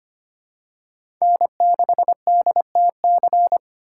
Morse code, heard as a keyed tone, about 25 wpm, 700 Hz. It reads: N6DTC
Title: USA Call Signs - 5 Characters - 6x Character Spacing - 25wpm